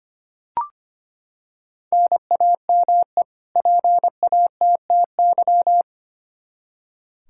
Morse code, heard as a keyed tone, about 25 wpm, 700 Hz. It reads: NAME PATTY
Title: K1USN SST QSO Practice 25wpm